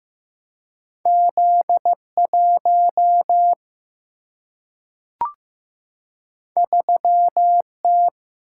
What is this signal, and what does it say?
Morse code, heard as a keyed tone, about 15 wpm, 700 Hz. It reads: Z1 3T
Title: Single-Letter Single-Number 15wpm